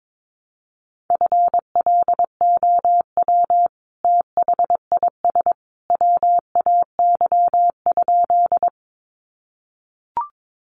Morse code, heard as a keyed tone, about 22 wpm, 700 Hz. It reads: FLOW THIS WAY?
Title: Sentences from Top 1200 Words 22wpm